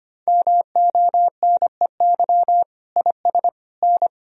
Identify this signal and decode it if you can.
Morse code, heard as a keyed tone, about 25 wpm, 700 Hz. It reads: MONEY IS N
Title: Sentences from Top 500 Words and Abbreviations 25wpm